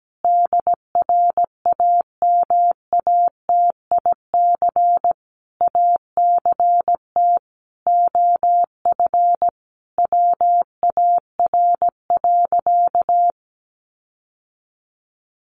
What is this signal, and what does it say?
Morse code, heard as a keyed tone, about 17 wpm, 700 Hz. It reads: DRAMATIC ACT OF WAR.
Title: Sentences from Top 1300 Words 17wpm